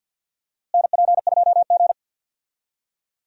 Morse code, heard as a keyed tone, about 50 wpm, 700 Hz. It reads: NP2G
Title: USA Call Signs - 4 Characters - Repeated 3x - 50wpm